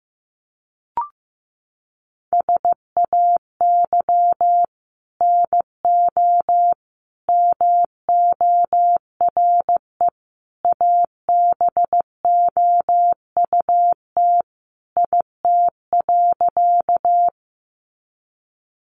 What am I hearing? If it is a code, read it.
Morse code, heard as a keyed tone, about 15 wpm, 700 Hz. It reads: SAY NO MORE ABOUT IT.
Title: Sentences from Top 200 Words 15wpm